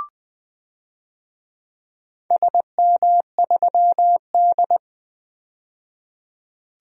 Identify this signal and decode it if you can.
Morse code, heard as a keyed tone, about 20 wpm, 700 Hz. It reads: SM3D
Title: DXCC-100 Call Signs - 4 Characters - 20wpm